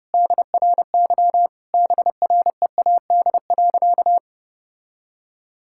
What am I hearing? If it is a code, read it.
Morse code, heard as a keyed tone, about 30 wpm, 700 Hz. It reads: DRY BREAD.
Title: Sentences from Top 1600 Words 30wpm